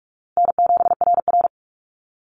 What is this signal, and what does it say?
Morse code, heard as a keyed tone, about 44 wpm, 700 Hz. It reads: N7RR
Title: Call Signs Repeated 2x - 45wpm